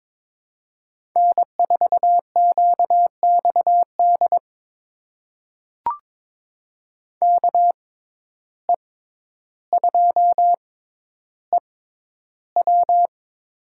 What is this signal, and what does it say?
Morse code, heard as a keyed tone, about 22 wpm, spaced as for 11 wpm, 700 Hz. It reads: N4QXD KE2EW
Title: USA Call Signs - 5 Characters - 6x Character Spacing - 22wpm